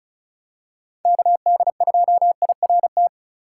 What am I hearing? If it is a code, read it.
Morse code, heard as a keyed tone, about 36 wpm, 700 Hz. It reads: KD2IRT
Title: USA Call Signs - 6 Characters - 2x Character Spacing - 35wpm